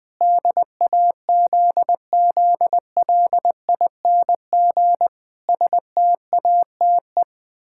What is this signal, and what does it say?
Morse code, heard as a keyed tone, about 20 wpm, 700 Hz. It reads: DAZZLING STATE
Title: Pangram Challenge v2 - 20wpm